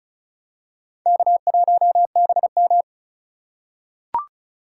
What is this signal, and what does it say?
Morse code, heard as a keyed tone, about 36 wpm, 700 Hz. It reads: K1BM
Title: USA Call Signs - 4 Characters - 4x Character Spacing - 35wpm